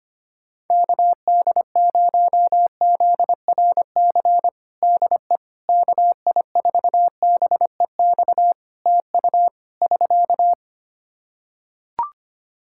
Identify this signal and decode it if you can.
Morse code, heard as a keyed tone, about 25 wpm, 700 Hz. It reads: KD0ZRC DE KI4BEX TU <SK>
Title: Practice QSO - Exchange 4 - Ending - 25wpm